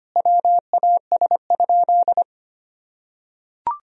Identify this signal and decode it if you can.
Morse code, heard as a keyed tone, about 25 wpm, 700 Hz. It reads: WAS?
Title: Sentences from Top 1000 Words - Review Part 3 - 25wpm